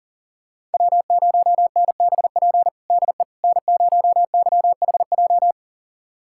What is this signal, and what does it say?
Morse code, heard as a keyed tone, about 40 wpm, 700 Hz. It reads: W0NBP DE N0YHJ
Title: USA Callsign DE Callsign - 5 Characters - 40wpm